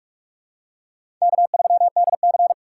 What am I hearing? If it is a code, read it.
Morse code, heard as a keyed tone, about 41 wpm, 700 Hz. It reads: K3DC
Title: USA Call Signs - 4 Characters - 2x Character Spacing - 45wpm